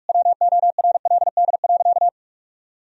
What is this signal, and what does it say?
Morse code, heard as a keyed tone, about 46 wpm, 700 Hz. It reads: WORLD.
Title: Sentences from Top 1000 Words - Review Part 4 - 45wpm